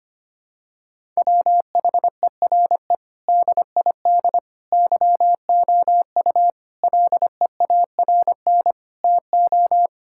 Morse code, heard as a keyed tone, about 25 wpm, 700 Hz. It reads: WHERE DID YOU LEARN TO
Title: Sentences from Top 1000 Words - Review Part 6 - 25wpm